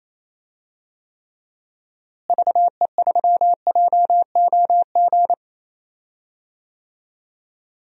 Morse code, heard as a keyed tone, about 28 wpm, 700 Hz. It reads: VE3JOG
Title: DXCC-100 Call Signs - 6 Characters - Repeated 3x - 28wpm